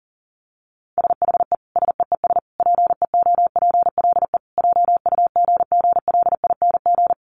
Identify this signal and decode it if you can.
Morse code, heard as a keyed tone, about 40 wpm, 700 Hz. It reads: SHE SEES PEOPLE JUGGLING
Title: Easy Phrases 40wpm